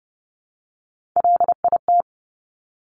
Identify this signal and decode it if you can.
Morse code, heard as a keyed tone, about 30 wpm, 700 Hz. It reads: LIT
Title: Top Three Letter Combinations in Words 30wpm